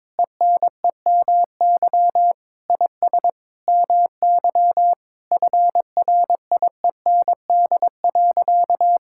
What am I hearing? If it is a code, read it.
Morse code, heard as a keyed tone, about 21 wpm, 700 Hz. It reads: ENEMY IS MY FRIEND.
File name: practice / 22wpm / Sentences with 8-Word Phrases - 22wpm